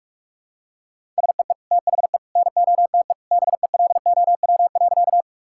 Morse code, heard as a keyed tone, about 44 wpm, 700 Hz. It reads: SEE THE NOTE BELOW.
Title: Sentences from Top 1000 Words - Review Part 5 - 45wpm